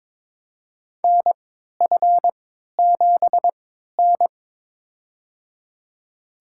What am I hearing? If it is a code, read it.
Morse code, heard as a keyed tone, about 23 wpm, spaced as for 13 wpm, 700 Hz. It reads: NF7N
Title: USA Call Signs - 4 Characters - 3x Character Spacing - 22wpm